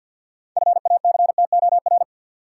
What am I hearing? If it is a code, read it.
Morse code, heard as a keyed tone, about 50 wpm, 700 Hz. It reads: FACTOR
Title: Sentences from Top 1300 Words - New Word Drill Alt - 50wpm